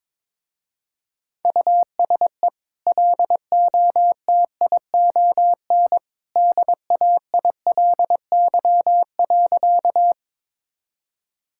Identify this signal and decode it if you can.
Morse code, heard as a keyed tone, about 22 wpm, 700 Hz. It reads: USE LOTION DAILY.
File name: practice / 22wpm / Easy Sentences with Hints - 22wpm